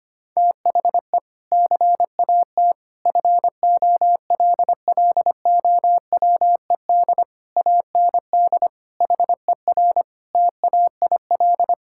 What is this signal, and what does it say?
Morse code, heard as a keyed tone, about 25 wpm, 700 Hz. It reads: THE CAT FOLLOWED AND HER TAIL
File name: practice / 25wpm / Sentences with 2-Word Phrases - 25wpm